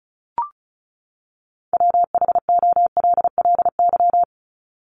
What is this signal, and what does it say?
Morse code, heard as a keyed tone, about 34 wpm, 700 Hz. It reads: WHOLLY
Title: LY Suffix - Top 100 Words with Base Words 35wpm